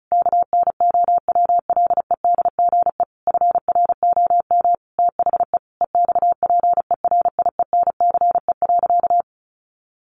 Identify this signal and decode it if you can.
Morse code, heard as a keyed tone, about 34 wpm, 700 Hz. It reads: KNOWLEDGE FROM THE EXPERIENCE.
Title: Sentences from Top 1300 Words 35wpm